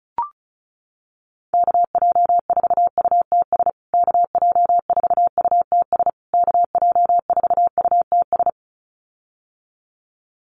Morse code, heard as a keyed tone, about 34 wpm, 700 Hz. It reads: KJ4UTS KJ4UTS KJ4UTS
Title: USA Call Signs Repeated 3x 35wpm